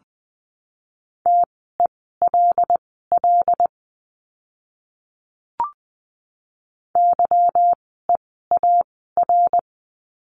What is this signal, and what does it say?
Morse code, heard as a keyed tone, about 20 wpm, spaced as for 14 wpm, 700 Hz. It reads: TELL YEAR
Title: 192 Lesson 31f - Letter v - Words 20wpm